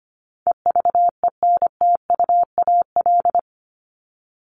Morse code, heard as a keyed tone, about 25 wpm, 700 Hz. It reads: EVENTUAL
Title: LY Suffix - Top 100 Words with Base Words 25wpm